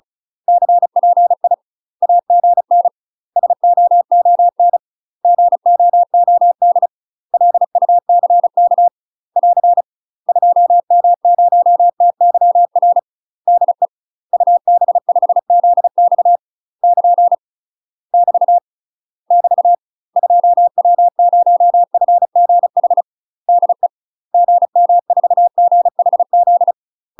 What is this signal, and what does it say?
Morse code, heard as a keyed tone, about 34 wpm, 700 Hz. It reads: CPI AGN SOON GOOD LUCK <AR> 2M0TYR DE UB5ZX <KN> <BT> <BT> 2W0FGH DE GM4GHZ
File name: UK MASTER QSOs TRAINING / 35WPM / UK MASTER QSOs 0039